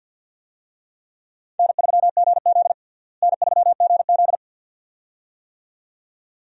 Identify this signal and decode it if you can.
Morse code, heard as a keyed tone, about 52 wpm, 700 Hz. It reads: N3GZ N3GZ
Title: USA Call Signs - 4 Characters - Repeated 2x - 50wpm